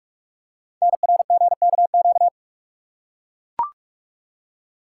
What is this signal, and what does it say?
Morse code, heard as a keyed tone, about 44 wpm, 690 Hz. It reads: NRGKQ
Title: Code Groups - 5 Letters - 45wpm